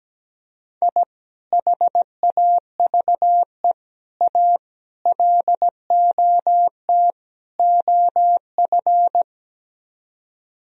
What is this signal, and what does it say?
Morse code, heard as a keyed tone, about 17 wpm, 700 Hz. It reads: I HAVE A LOT OF
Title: Speed-Racing - Sets of 5 Words 17wpm